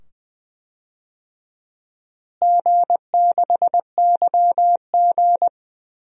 Morse code, heard as a keyed tone, about 20 wpm, 700 Hz. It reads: G6YG
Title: DXCC-100 Call Signs - 4 Characters - Repeated 2x - 20wpm